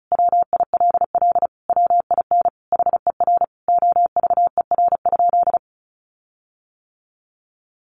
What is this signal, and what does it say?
Morse code, heard as a keyed tone, about 34 wpm, 700 Hz. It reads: WILL WIN HER OVER?
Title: Sentences from Top 1500 Words 35wpm